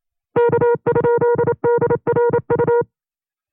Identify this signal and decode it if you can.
Morse code, heard as a keyed tone, about 28 wpm, 480 Hz. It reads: K?DRU
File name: 25WPM